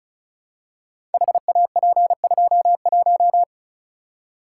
Morse code, heard as a keyed tone, about 35 wpm, 690 Hz. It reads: HAP21
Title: Code Groups - 5 Letter:Numbers - 35wpm